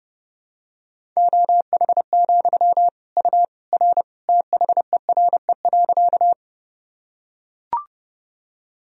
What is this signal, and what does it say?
Morse code, heard as a keyed tone, about 30 wpm, 700 Hz. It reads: OH, U R THERE.
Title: Sentences from Top 500 Words and Abbreviations 30wpm